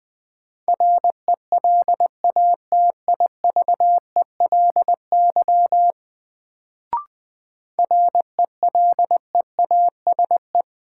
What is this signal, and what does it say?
Morse code, heard as a keyed tone, about 21 wpm, 700 Hz. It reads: RELATIVELY RELEASE
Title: Sentences from Top 1100 Words - New Word Drill Alt - 20wpm